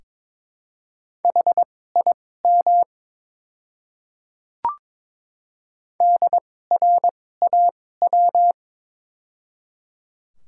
Morse code, heard as a keyed tone, about 22 wpm, spaced as for 16 wpm, 700 Hz. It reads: HIM DRAW
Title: 2x Character Spacing - Top 500 Words - 22wpm